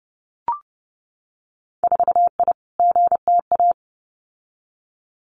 Morse code, heard as a keyed tone, about 30 wpm, 700 Hz. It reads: 4I GTA
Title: Winter Field Day Exchanges Encore 30wpm